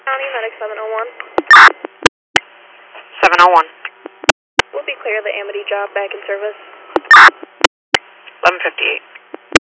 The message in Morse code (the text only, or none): none